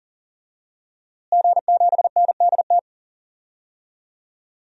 Morse code, heard as a keyed tone, about 38 wpm, 700 Hz. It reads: G7NDT
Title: DXCC-100 Call Signs - 5 Characters - Repeated 2x - 40wpm